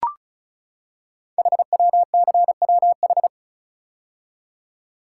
Morse code, heard as a keyed tone, about 36 wpm, 690 Hz. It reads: HWCWH
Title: Code Groups - 5 Letters - 35wpm